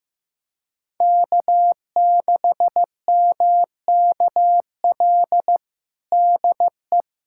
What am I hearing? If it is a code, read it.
Morse code, heard as a keyed tone, about 16 wpm, 700 Hz. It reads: K6MKL DE
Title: USA Callsign DE Callsign - 5 Characters - 15wpm